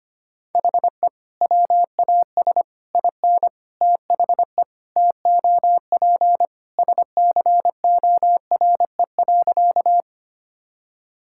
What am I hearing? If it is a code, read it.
Morse code, heard as a keyed tone, about 25 wpm, 700 Hz. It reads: HE WAS IN THE TOP SCORE.
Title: Sentences from Top 1600 Words 25wpm